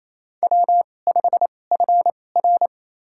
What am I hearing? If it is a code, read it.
Morse code, heard as a keyed tone, about 28 wpm, spaced as for 20 wpm, 700 Hz. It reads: W5FR
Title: USA Call Signs - 4 Characters - 2x Character Spacing - 28wpm